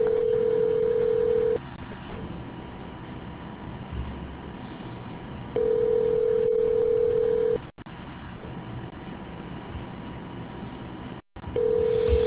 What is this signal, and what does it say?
Ringback tone (North America), 440 + 480 Hz.